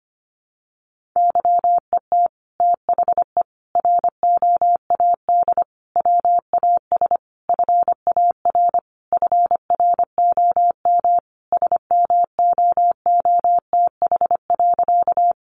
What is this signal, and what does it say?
Morse code, heard as a keyed tone, about 25 wpm, 700 Hz. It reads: YET THE ROAD WAS FAR FROM SMOOTH.
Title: Sentences from Top 1400 Words 25wpm